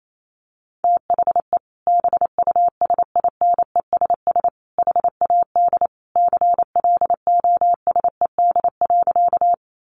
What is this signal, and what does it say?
Morse code, heard as a keyed tone, about 28 wpm, 700 Hz. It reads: THE BUSINESS HAD CLOSED.
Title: Sentences from Top 1400 Words 28wpm